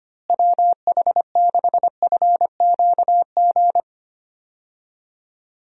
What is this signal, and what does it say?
Morse code, heard as a keyed tone, about 25 wpm, 700 Hz. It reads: WH6FQG
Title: USA Call Signs - 6 Characters - Repeated 3x - 25wpm